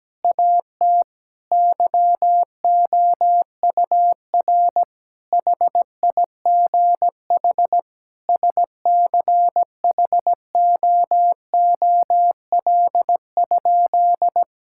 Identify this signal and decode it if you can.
Morse code, heard as a keyed tone, about 17 wpm, 700 Hz. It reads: AT YOUR HIGH SCHOOL?
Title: Sentences from Top 1000 Words - Review Part 6 - 17wpm